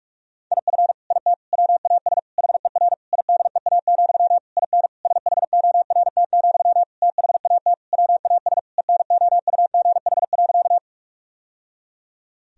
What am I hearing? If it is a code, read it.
Morse code, heard as a keyed tone, about 46 wpm, 700 Hz. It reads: IF IT WAS HER IDEA, IN SHORT, THAT WAS ENOUGH.
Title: Sentences from Top 1000 Words - Review Part 2 - 45wpm